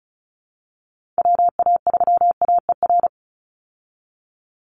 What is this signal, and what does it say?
Morse code, heard as a keyed tone, about 34 wpm, 700 Hz. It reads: WA3AER
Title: Call Signs Repeated 2x - 35wpm